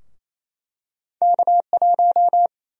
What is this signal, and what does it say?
Morse code, heard as a keyed tone, about 28 wpm, 700 Hz. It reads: K1
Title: USA Call Sign Prefixes - Repeated 2x - 28wpm